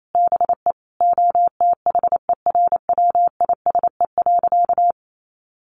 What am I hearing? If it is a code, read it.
Morse code, heard as a keyed tone, about 28 wpm, 700 Hz. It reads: BE OTHERWISE.
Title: Sentences from Top 1200 Words 28wpm